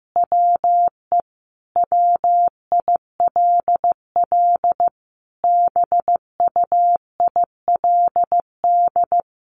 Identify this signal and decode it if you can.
Morse code, heard as a keyed tone, about 15 wpm, 700 Hz. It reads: WE WILL BUILD
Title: Sentences from Top 1400 Words 15wpm